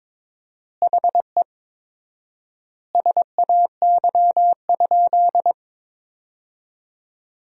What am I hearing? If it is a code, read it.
Morse code, heard as a keyed tone, about 21 wpm, 700 Hz. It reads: HE SAY?
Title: 4x Word-Spacing - Sentences from Top 300 Words - 22wpm